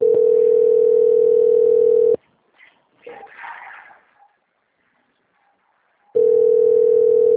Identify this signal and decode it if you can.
Ringback tone (North America), 440 + 480 Hz.